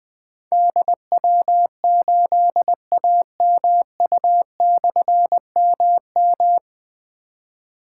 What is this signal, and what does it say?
Morse code, heard as a keyed tone, about 20 wpm, 700 Hz. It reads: DW8AMU/MM
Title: Difficult Call Signs 20wpm